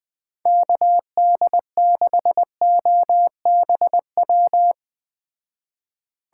Morse code, heard as a keyed tone, about 20 wpm, 700 Hz. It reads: KD6OBW
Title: Speed Racing - USA 2x3 Call Signs 17wpm